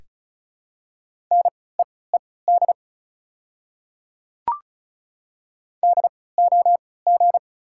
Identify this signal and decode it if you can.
Morse code, heard as a keyed tone, about 34 wpm, spaced as for 20 wpm, 690 Hz. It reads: NEED DOG
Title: 3x Character Spacing - Top 500 Words - 35wpm